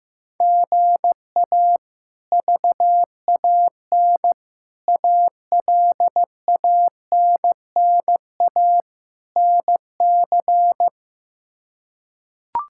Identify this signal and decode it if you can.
Morse code, heard as a keyed tone, about 15 wpm, 700 Hz. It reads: GA VAN ALANNA NC
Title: K1USN SST QSO Practice - 15wpm